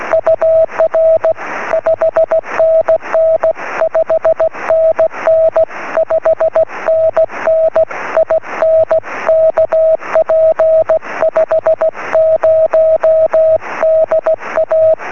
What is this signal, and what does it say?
Morse code, heard as a keyed tone, about 16 wpm, 650 Hz. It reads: UR 5NN 5NN 5NN IN KP50DA
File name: sm7lcb / ua1ali 23cm